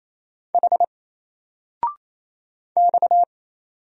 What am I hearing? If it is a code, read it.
Morse code, heard as a keyed tone, about 27 wpm, 700 Hz. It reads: H X
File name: Single-Letter-Number-28wpm